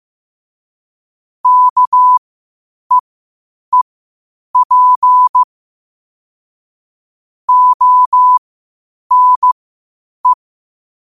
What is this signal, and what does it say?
Morse code, heard as a keyed tone, about 15 wpm, spaced as for 8 wpm, 1000 Hz. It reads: KEEP ONE